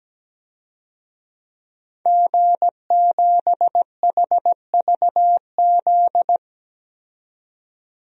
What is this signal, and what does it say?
Morse code, heard as a keyed tone, about 17 wpm, 700 Hz. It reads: G7HVZ
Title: DXCC-100 Call Signs - 5 Characters - 17wpm